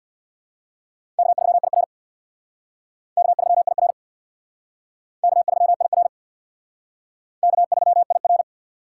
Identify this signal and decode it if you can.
Morse code, heard as a keyed tone, about 60 wpm, 700 Hz. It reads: K3IR K3IR K3IR K3IR
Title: Speed Racing - USA 1x2 Call Signs 50wpm